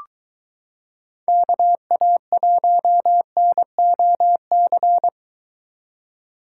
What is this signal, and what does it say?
Morse code, heard as a keyed tone, about 23 wpm, 700 Hz. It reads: KA1NOC